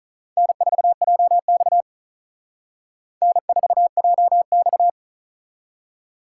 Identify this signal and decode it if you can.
Morse code, heard as a keyed tone, about 45 wpm, 700 Hz. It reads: N4JX N4JX